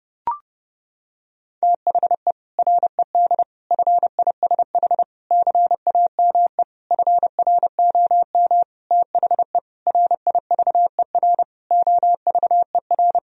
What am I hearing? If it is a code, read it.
Morse code, heard as a keyed tone, about 30 wpm, 700 Hz. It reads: THE RED FISH CAME FROM THE RIVER OVER
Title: Sentences from Top 1000 Words - Review Part 2 - 30wpm